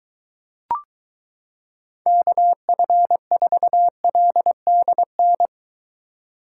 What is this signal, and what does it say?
Morse code, heard as a keyed tone, about 22 wpm, 700 Hz. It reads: KF4LDN